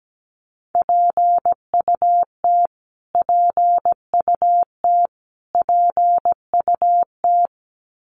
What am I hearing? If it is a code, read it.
Morse code, heard as a keyed tone, about 17 wpm, 700 Hz. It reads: PUT PUT PUT
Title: Top 500 Words Repeated 17wpm